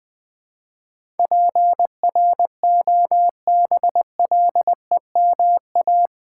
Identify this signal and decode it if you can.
Morse code, heard as a keyed tone, about 19 wpm, 700 Hz. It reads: PROBLEMA
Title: Build A Word - 4 Letters At A Time - 11-Letter Words - 20wpm